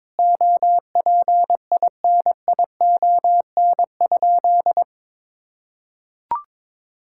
Morse code, heard as a keyed tone, about 21 wpm, 700 Hz. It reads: OPINION?
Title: Sentences from Top 1600 Words 22wpm